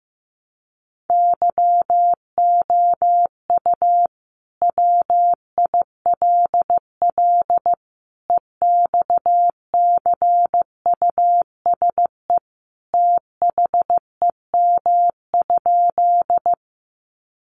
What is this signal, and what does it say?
Morse code, heard as a keyed tone, about 15 wpm, 700 Hz. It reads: YOU WILL EXCUSE THEM?
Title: Sentences from Top 700 Words 15wpm